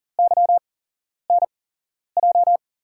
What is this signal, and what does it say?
Morse code, heard as a keyed tone, about 40 wpm, 700 Hz. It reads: Y N J
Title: USA Call Signs - 6 Characters - 8x Character Spacing - 40wpm